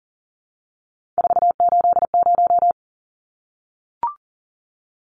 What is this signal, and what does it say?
Morse code, heard as a keyed tone, about 40 wpm, 700 Hz. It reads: 480
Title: Numbers - 3-Digit 40wpm